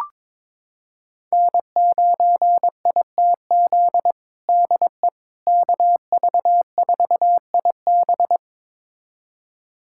Morse code, heard as a keyed tone, about 22 wpm, 700 Hz. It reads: N9ITZ DE KV4IB
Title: USA Callsign DE Callsign - 5 Characters - 22wpm